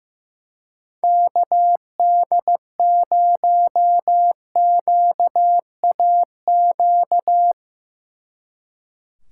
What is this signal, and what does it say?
Morse code, heard as a keyed tone, about 18 wpm, 700 Hz. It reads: KD0QAQ
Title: Speed Racing - USA 2x3 Call Signs 15wpm